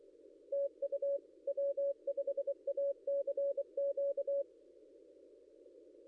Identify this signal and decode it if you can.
Morse code, heard as a keyed tone, about 23 wpm, 560 Hz. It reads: TU W5ACQ
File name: contests / ss-cw-k5tr-op-n5rz -0800